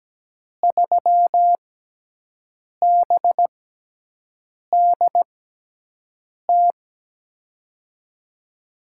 Morse code, heard as a keyed tone, about 17 wpm, spaced as for 6 wpm, 700 Hz. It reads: 3BDT
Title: USA Call Signs - 5 Characters - 6x Character Spacing - 17wpm